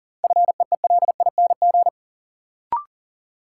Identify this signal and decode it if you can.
Morse code, heard as a keyed tone, about 40 wpm, 700 Hz. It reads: FEELING
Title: Sentences from Top 1000 Words - Review Part 3 - New Word Drill - 40wpm